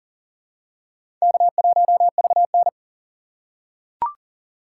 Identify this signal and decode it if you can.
Morse code, heard as a keyed tone, about 40 wpm, 700 Hz. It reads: K1VN
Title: USA Call Signs - 2x Character Spacing - 40wpm